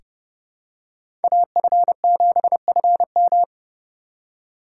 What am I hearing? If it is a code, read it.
Morse code, heard as a keyed tone, about 30 wpm, 700 Hz. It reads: AF7FM
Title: USA Call Signs - 5 Characters - 2x Character Spacing - 30wpm